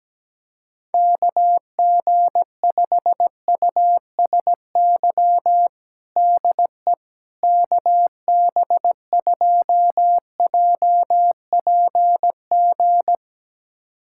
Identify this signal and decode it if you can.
Morse code, heard as a keyed tone, about 17 wpm, 700 Hz. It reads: KG5USY DE KB2JPG
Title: USA Callsign DE Callsign - 6 Characters - 17wpm